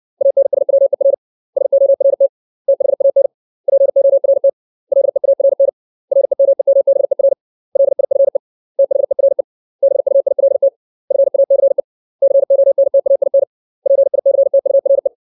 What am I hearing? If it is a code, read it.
Morse code, heard as a keyed tone, about 60 wpm, 550 Hz. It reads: ANSWER SORT THAN PORT LEARN REMEMBER LIFE THERE BUILT FACE CONTAIN PICTURE
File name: 632 words / 632 Words 60WPM